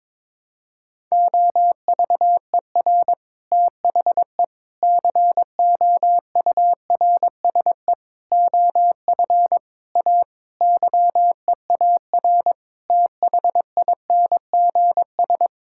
Morse code, heard as a keyed tone, about 22 wpm, 700 Hz. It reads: OVER THE COURSE OF A YEAR THINGS
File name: practice / 22wpm / Sentences with 4-Word Phrases - 22wpm